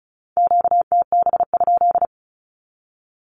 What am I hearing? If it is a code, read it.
Morse code, heard as a keyed tone, about 34 wpm, 700 Hz. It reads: QTB?
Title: Q-Codes 35wpm